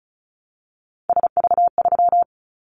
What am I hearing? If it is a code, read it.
Morse code, heard as a keyed tone, about 36 wpm, 700 Hz. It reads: SV3
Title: Callsign Prefixes for Top 50 DXCC - 35wpm